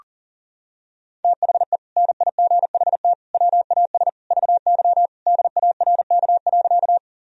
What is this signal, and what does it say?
Morse code, heard as a keyed tone, about 40 wpm, 700 Hz. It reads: THE NIGHT WAS VY DARK.
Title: Sentences from Top 500 Words and Abbreviations 40wpm